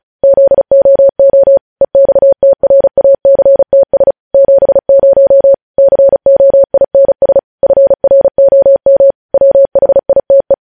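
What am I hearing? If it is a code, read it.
Morse code, heard as a keyed tone, about 36 wpm, 550 Hz. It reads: ZOO EXTRACTS 70 COINS FROM WHITE
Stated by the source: Morse code news headlines sent at 35, beginning with vvv.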